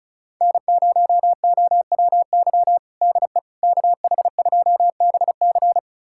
Morse code, heard as a keyed tone, about 36 wpm, 700 Hz. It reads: N0OWY DE KH2BC
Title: USA Callsign DE Callsign - 5 Characters - 35wpm